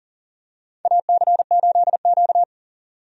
Morse code, heard as a keyed tone, about 40 wpm, 700 Hz. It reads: AC8Q
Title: USA Call Signs - 4 Characters - 40wpm